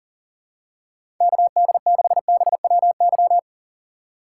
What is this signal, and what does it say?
Morse code, heard as a keyed tone, about 40 wpm, 700 Hz. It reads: KD6BWY
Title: USA Call Signs - 3x Character Spacing - 40wpm